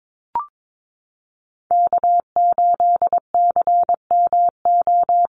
Morse code, heard as a keyed tone, about 22 wpm, 690 Hz. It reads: K8CMO
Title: Call Signs Repeated 3x - 22wpm